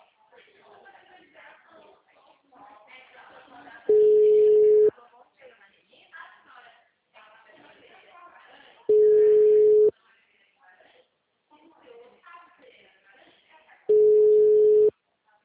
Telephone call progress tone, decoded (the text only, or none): ringback tone